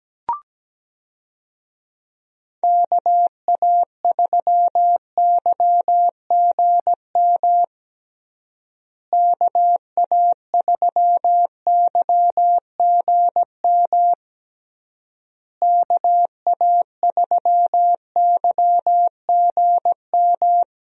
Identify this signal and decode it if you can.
Morse code, heard as a keyed tone, about 17 wpm, 700 Hz. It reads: KA3YGM KA3YGM KA3YGM
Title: USA Call Signs - 6 Characters - Repeated 3x - 17wpm